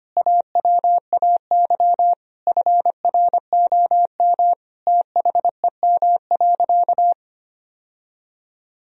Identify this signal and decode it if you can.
Morse code, heard as a keyed tone, about 25 wpm, 700 Hz. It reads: AWAY FROM THEM.
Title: Sentences from Top 1000 Words - Review Part 2 - 25wpm